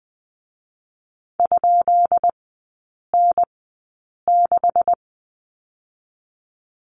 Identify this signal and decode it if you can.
Morse code, heard as a keyed tone, about 20 wpm, spaced as for 8 wpm, 690 Hz. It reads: ?N6
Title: 172 Lesson 28e - Number 9 - Three Characters 20wpm